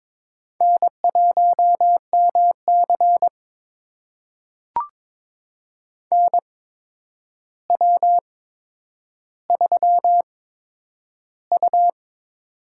Morse code, heard as a keyed tone, about 21 wpm, spaced as for 9 wpm, 700 Hz. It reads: N1MC NW3U
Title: USA Call Signs - 4 Characters - 8x Character Spacing - 22wpm